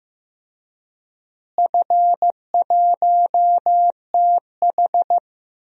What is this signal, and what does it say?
Morse code, heard as a keyed tone, about 15 wpm, 700 Hz. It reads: F1TH
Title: DXCC-100 Call Signs - 4 Characters - Repeated 3x - 15wpm